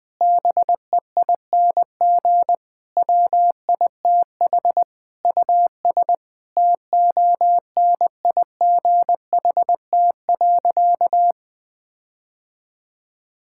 Morse code, heard as a keyed tone, about 20 wpm, 700 Hz. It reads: BEING WITH US TONIGHT.
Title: Sentences with 5-Word Phrases - Spoken Beginning - 20wpm